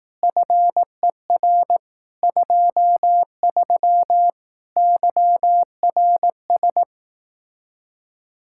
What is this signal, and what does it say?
Morse code, heard as a keyed tone, about 18 wpm, 700 Hz. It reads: FER 23 YRS
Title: Speed-Racing Common QSO Phrases Encore - 15wpm